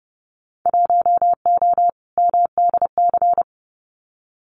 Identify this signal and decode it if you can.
Morse code, heard as a keyed tone, about 30 wpm, 700 Hz. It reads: 1O MDC
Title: Winter Field Day Exchanges 30wpm